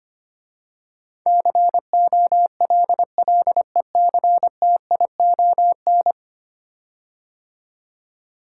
Morse code, heard as a keyed tone, about 25 wpm, 700 Hz. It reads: COLLECTION
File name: 10 Letter Words - First 3 Letters Given - 25wpm